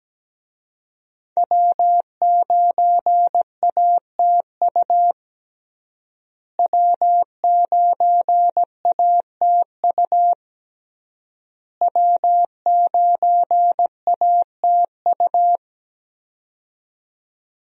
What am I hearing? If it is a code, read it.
Morse code, heard as a keyed tone, about 17 wpm, 700 Hz. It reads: W9ATU W9ATU W9ATU
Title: USA Call Signs - 5 Characters - Repeated 3x - 17wpm